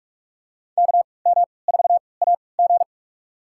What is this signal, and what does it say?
Morse code, heard as a keyed tone, about 47 wpm, spaced as for 26 wpm, 700 Hz. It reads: KM4AG
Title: USA Call Signs - 5 Characters - 3x Character Spacing - 45wpm